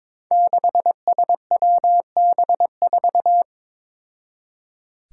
Morse code, heard as a keyed tone, about 22 wpm, 700 Hz. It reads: 6SWB4
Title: Five Challenging Letter-Number Combinations 22wpm